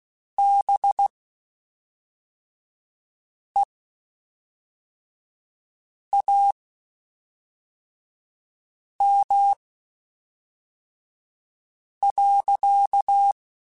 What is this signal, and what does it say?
Morse code, heard as a keyed tone, about 16 wpm, spaced as for 3 wpm, 800 Hz. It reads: BEAM.
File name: (spaced) / QSOs Spaced-mp3s / WD0BGP 1 @5 sp